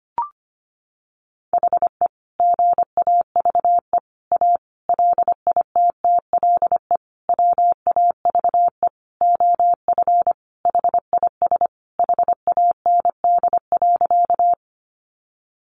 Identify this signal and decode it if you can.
Morse code, heard as a keyed tone, about 25 wpm, 700 Hz. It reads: HE GAVE A LITTLE WAVE OF HIS HAND.
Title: Sentences from Top 1400 Words 25wpm